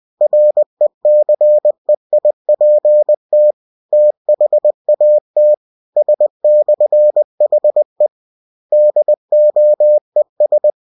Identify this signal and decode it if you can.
Morse code, heard as a keyed tone, about 20 wpm, 600 Hz. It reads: RECEIPT THAT S/HE DOES
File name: cw-books / winnie-the-pooh / winnie-20wpm-page-0117